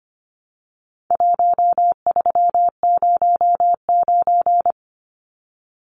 Morse code, heard as a keyed tone, about 25 wpm, 700 Hz. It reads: 1309
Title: Numbers - 4-Digit 25wpm